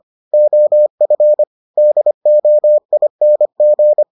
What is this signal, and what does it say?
Morse code, heard as a keyed tone, about 25 wpm, 600 Hz. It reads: OF DOING